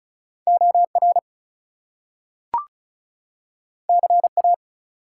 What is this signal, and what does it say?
Morse code, heard as a keyed tone, about 36 wpm, 700 Hz. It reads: OR CA
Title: US States and Canadian Provinces 35wpm